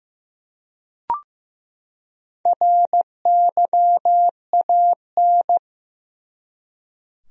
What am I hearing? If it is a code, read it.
Morse code, heard as a keyed tone, about 15 wpm, 700 Hz. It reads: RYAN
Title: USA Common Names - Max 4 Letters - 15wpm